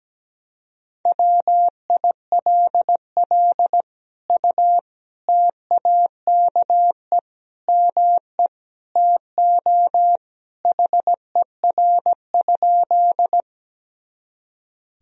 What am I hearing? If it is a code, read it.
Morse code, heard as a keyed tone, about 17 wpm, 700 Hz. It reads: WILL U TAKE ME TO HER?
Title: Sentences from Top 200 Words and Abbreviations 17wpm